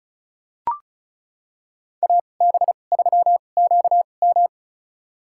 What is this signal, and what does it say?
Morse code, heard as a keyed tone, about 34 wpm, spaced as for 25 wpm, 700 Hz. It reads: AB3QM
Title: USA Call Signs - 5 Characters - 2x Character Spacing - 35wpm